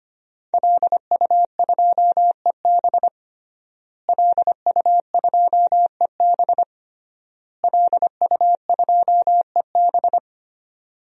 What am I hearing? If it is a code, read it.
Morse code, heard as a keyed tone, about 25 wpm, 700 Hz. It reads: LU2EB LU2EB LU2EB
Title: DXCC-100 Call Signs - 5 Characters - Repeated 3x - 25wpm